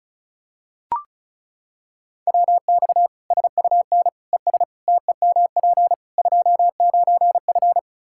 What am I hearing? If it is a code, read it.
Morse code, heard as a keyed tone, about 34 wpm, 700 Hz. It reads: WX SUN ES TEMP 29F
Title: K1USN SST QSO Practice 35wpm